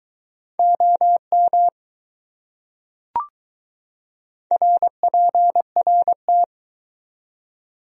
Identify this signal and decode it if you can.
Morse code, heard as a keyed tone, about 21 wpm, 700 Hz. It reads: OM RPRT